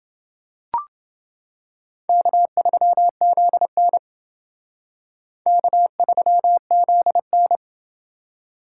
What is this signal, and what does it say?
Morse code, heard as a keyed tone, about 29 wpm, 700 Hz. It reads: K3ZN K3ZN